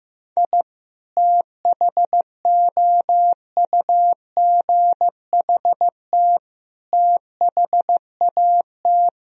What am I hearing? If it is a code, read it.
Morse code, heard as a keyed tone, about 15 wpm, 700 Hz. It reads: I THOUGHT THAT
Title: Sentences from Top 1000 Words - Review Part 4 - 15wpm